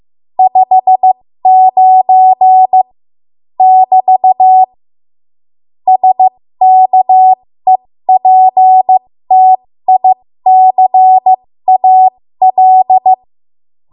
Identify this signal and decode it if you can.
Morse code, heard as a keyed tone, about 15 wpm, spaced as for 13 wpm, 750 Hz. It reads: 59 <BT> SKEPTICAL
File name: Archive / 13 WPM / 13WPM